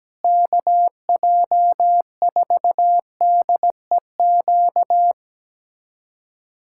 Morse code, heard as a keyed tone, about 17 wpm, 700 Hz. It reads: KJ4DEQ
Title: USA Callsign DE Callsign 17wpm